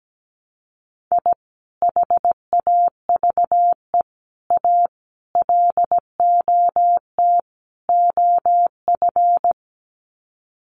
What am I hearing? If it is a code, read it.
Morse code, heard as a keyed tone, about 17 wpm, 700 Hz. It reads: I HAVE A LOT OF
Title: Sets of 5 Words 17wpm